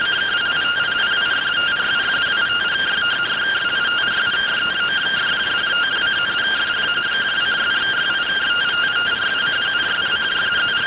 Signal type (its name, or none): two-tone teleprinter signal